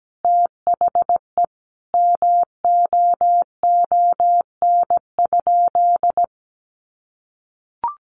Morse code, heard as a keyed tone, about 17 wpm, 700 Hz. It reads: THE MOON?
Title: Sentences from Top 1200 Words 17wpm